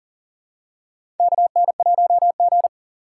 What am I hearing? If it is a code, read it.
Morse code, heard as a keyed tone, about 40 wpm, 700 Hz. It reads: KN1G
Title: USA Call Signs - 3x Character Spacing - 40wpm